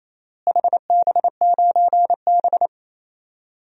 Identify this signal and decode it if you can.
Morse code, heard as a keyed tone, about 28 wpm, 700 Hz. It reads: HB9B
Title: DXCC-100 Call Signs - 4 Characters - Repeated 2x - 28wpm